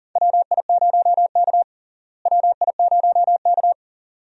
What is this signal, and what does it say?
Morse code, heard as a keyed tone, about 37 wpm, 700 Hz. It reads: WI0K WI0K
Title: USA Call Signs - 4 Characters - Repeated 2x - 40wpm